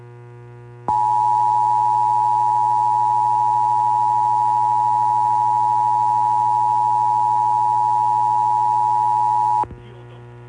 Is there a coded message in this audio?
Yes: Emergency Alert System (SAME) tones (the 853 and 960 Hz attention tone) from 0.8 to 9.7 s.